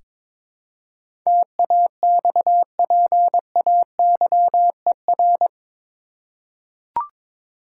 Morse code, heard as a keyed tone, about 22 wpm, 700 Hz. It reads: TAXPAYER
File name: practice / 22wpm / Eight Letter Words - 22wpm